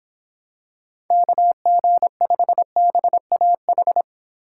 Morse code, heard as a keyed tone, about 26 wpm, 700 Hz. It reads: KG5BAH
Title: Speed Racing - USA 2x3 Call Signs 22wpm